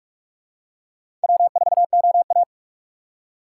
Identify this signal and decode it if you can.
Morse code, heard as a keyed tone, about 46 wpm, 700 Hz. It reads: W4OA
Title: USA Call Signs - 4 Characters - 3x Character Spacing - 45wpm